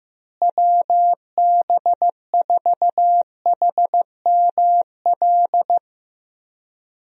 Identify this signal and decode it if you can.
Morse code, heard as a keyed tone, about 15 wpm, 700 Hz. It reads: WB4HML
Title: USA Call Signs - 6 Characters - Repeated 2x - 15wpm